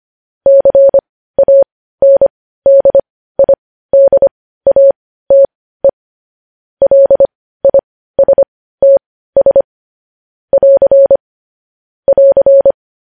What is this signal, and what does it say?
Morse code, heard as a keyed tone, about 25 wpm, spaced as for 15 wpm, 550 Hz. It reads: CANDIDATE LISTS <AR> <AR>
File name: podcast / News headlines-1x-25x15